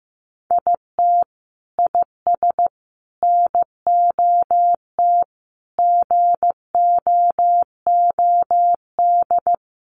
Morse code, heard as a keyed tone, about 15 wpm, 700 Hz. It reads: IT IS NOT GOOD
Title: Sentences from Top 600 Words 15wpm